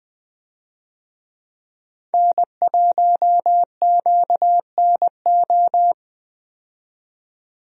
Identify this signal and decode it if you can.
Morse code, heard as a keyed tone, about 21 wpm, 700 Hz. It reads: N1QNO
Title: USA Call Signs - 5 Characters - 20wpm